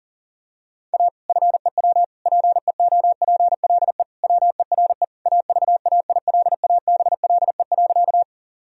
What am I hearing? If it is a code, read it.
Morse code, heard as a keyed tone, about 40 wpm, 700 Hz. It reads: A FEW PEOPLE WERE AVAILABLE.
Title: Sentences from Top 1000 Words - Review Part 3 - 40wpm